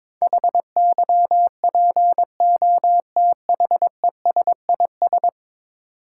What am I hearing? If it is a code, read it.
Morse code, heard as a keyed tone, about 22 wpm, 700 Hz. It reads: HYPOTHESIS
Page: morse-code-ninja-practice-22wpm